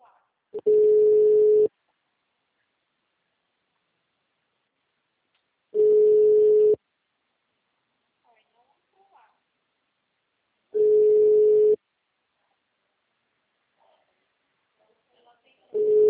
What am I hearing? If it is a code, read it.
Ringback tone (Europe, 425 Hz), with speech.
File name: out- -2026- - - .52219